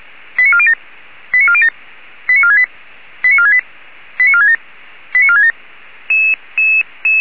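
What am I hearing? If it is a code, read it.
Five-tone selective call (ZVEI), heard as quick runs of five tones, tones of about 70 ms. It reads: 89407 89407 89468 89468 89467 89467
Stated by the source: ZVEI five-tone call received by SDR.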